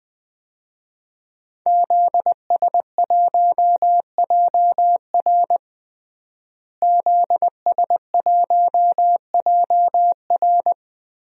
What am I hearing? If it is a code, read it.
Morse code, heard as a keyed tone, about 20 wpm, 700 Hz. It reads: ZS1JR ZS1JR
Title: DXCC-100 Call Signs - 5 Characters - Repeated 3x - 20wpm